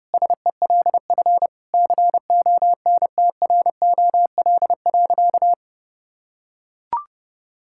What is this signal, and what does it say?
Morse code, heard as a keyed tone, about 30 wpm, 700 Hz. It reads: SELF CONTROL.
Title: Sentences from Top 1000 Words - Review Part 5 - 30wpm